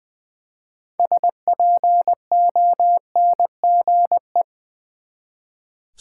Morse code, heard as a keyed tone, about 20 wpm, 700 Hz. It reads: SPONGE
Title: Easy Words with Hints - 20wpm